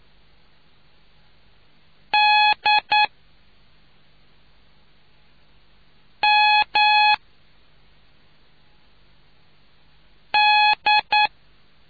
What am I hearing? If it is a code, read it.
Morse code, heard as a keyed tone, about 9 wpm, 840 Hz. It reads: D M D